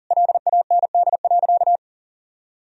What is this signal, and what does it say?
Morse code, heard as a keyed tone, about 40 wpm, 700 Hz. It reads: LAND.
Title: Sentences from Top 1000 Words - Review Part 4 - 40wpm